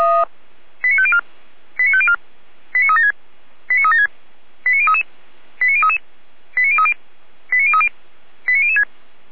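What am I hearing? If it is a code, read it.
Five-tone selective call (ZVEI), heard as quick runs of five tones, tones of about 70 ms. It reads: 89504 89504 89376 89376 89033 89033 89030 89030 89006
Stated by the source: Received on an SDR, ZVEI five-tone selective call, channel 455.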